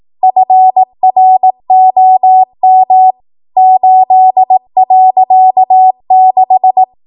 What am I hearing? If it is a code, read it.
Morse code, heard as a keyed tone, about 18 wpm, 750 Hz. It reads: FROM 8.6
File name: Archive / 18 WPM / 18WPM